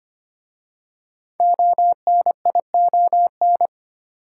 Morse code, heard as a keyed tone, about 25 wpm, 700 Hz. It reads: ONION
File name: practice / 25wpm / Difficult Words with Hints - 25wpm